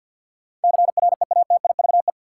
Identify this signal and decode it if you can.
Morse code, heard as a keyed tone, about 51 wpm, 700 Hz. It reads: CREATIVE
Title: Sentences from Top 1000 Words - Review Part 5 - New Word Drill - 50wpm